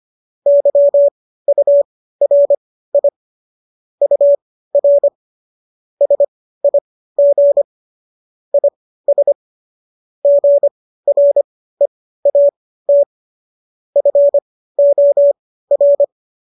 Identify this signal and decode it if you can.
Morse code, heard as a keyed tone, about 25 wpm, spaced as for 15 wpm, 570 Hz. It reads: YURI UR SIG IS GREAT FOR